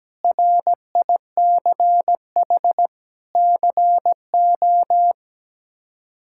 Morse code, heard as a keyed tone, about 17 wpm, 700 Hz. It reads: RICH CO
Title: K1USN SST QSO Practice - 17wpm